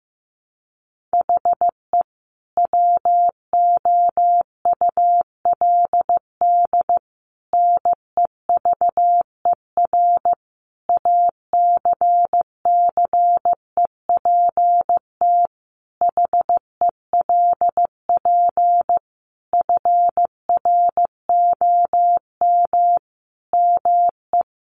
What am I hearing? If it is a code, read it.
Morse code, heard as a keyed tone, about 15 wpm, 700 Hz. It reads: HE WOULD NEVER ACCEPT HELP FROM ME
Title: Sentences from Top 1300 Words 15wpm